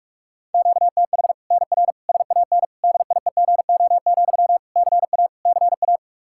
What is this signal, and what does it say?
Morse code, heard as a keyed tone, about 46 wpm, 700 Hz. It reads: QTH NR SAN DIEGO, CA CA
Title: Practice QSO - Exchange 1 - Beginning - 45wpm